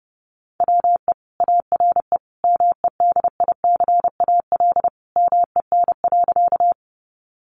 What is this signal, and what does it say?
Morse code, heard as a keyed tone, about 30 wpm, 700 Hz. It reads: WE ARE MEDICAL MEN.
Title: Sentences from Top 800 Words 30wpm